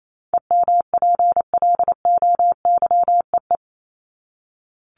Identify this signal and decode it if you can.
Morse code, heard as a keyed tone, about 28 wpm, 700 Hz. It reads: EMPLOYEE
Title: Top 1501-1750 Words 28wpm